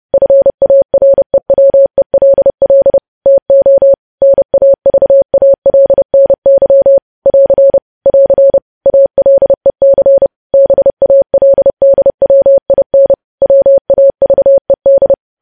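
Morse code, heard as a keyed tone, about 30 wpm, 550 Hz. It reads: FAREWELL TO NAVALNY <AR> <AR> ALEC BALDWIN WAVED
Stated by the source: Morse code news headlines sent at 30 words per minute.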